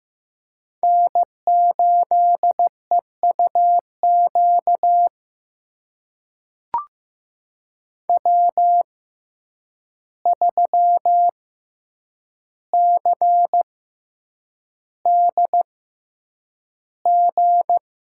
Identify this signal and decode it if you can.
Morse code, heard as a keyed tone, about 15 wpm, spaced as for 8 wpm, 700 Hz. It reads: N8EUQ W3CDG
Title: USA Call Signs - 5 Characters - 6x Character Spacing - 15wpm